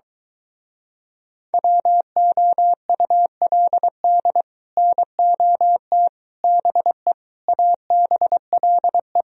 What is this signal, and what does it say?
Morse code, heard as a keyed tone, about 22 wpm, 700 Hz. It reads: WOULD NOT BE ABLE